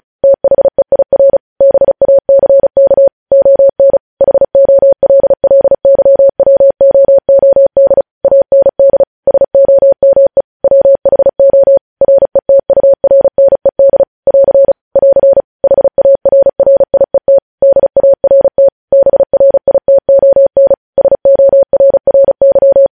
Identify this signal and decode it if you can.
Morse code, heard as a keyed tone, about 36 wpm, 550 Hz. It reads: THEIR BACK ON HOLLYWOOD AND SOME WHO RETURNED <AR> <AR> HARRIET DART BRITON SORRY
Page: news headlines in morse code at 35 wpm